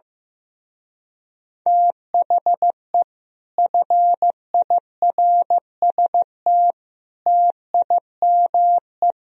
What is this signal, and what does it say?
Morse code, heard as a keyed tone, about 15 wpm, 700 Hz. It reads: THE FIRST TIME
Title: Speed-Racing - Sets of 4 Words 15wpm